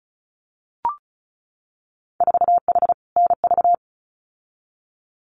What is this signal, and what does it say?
Morse code, heard as a keyed tone, about 36 wpm, 700 Hz. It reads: 4H NV
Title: Winter Field Day Exchanges Encore 35wpm